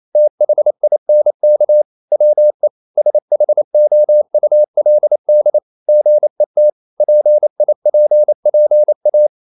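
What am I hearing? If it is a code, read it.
Morse code, heard as a keyed tone, about 28 wpm, 600 Hz. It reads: THINK WE SHOULD GET PIPPA
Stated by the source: Morse code practice at 28 wpm, text of Ken Follett's novel Never in English, from CW med Gnister.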